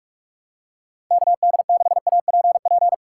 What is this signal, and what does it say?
Morse code, heard as a keyed tone, about 46 wpm, 700 Hz. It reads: KD6APP
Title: USA Call Signs - 6 Characters - 3x Character Spacing - 45wpm